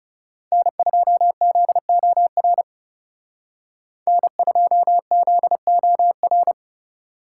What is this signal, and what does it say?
Morse code, heard as a keyed tone, about 36 wpm, 700 Hz. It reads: N2ZOR N2ZOR
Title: Speed Racing - USA Call Signs 30wpm